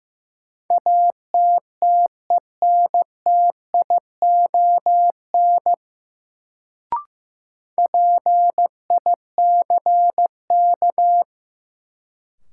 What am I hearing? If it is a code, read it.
Morse code, heard as a keyed tone, about 15 wpm, 700 Hz. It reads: ATTENTION PICK
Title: Sentences from Top 1000 Words - New Word Drill - 15wpm